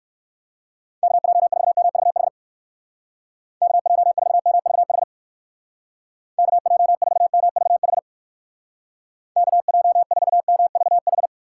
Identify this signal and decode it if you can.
Morse code, heard as a keyed tone, about 58 wpm, 700 Hz. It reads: KJ4MVH KJ4MVH KJ4MVH KJ4MVH